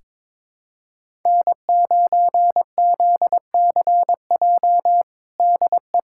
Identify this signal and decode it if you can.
Morse code, heard as a keyed tone, about 22 wpm, 700 Hz. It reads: N9ZCJ DE
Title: USA Callsign DE Callsign - 5 Characters - 22wpm